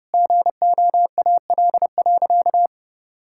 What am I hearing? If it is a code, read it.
Morse code, heard as a keyed tone, about 30 wpm, 700 Hz. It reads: GOAL.
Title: Sentences from Top 1000 Words - Review Part 6 - 30wpm